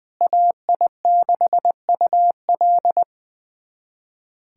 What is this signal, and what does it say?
Morse code, heard as a keyed tone, about 20 wpm, 700 Hz. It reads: AI6UL
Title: USA Callsign DE Callsign - 5 Characters - 20wpm